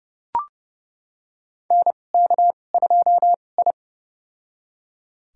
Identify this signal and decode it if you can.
Morse code, heard as a keyed tone, about 30 wpm, spaced as for 22 wpm, 700 Hz. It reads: NK2I
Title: USA Call Signs - 2x Character Spacing - 30wpm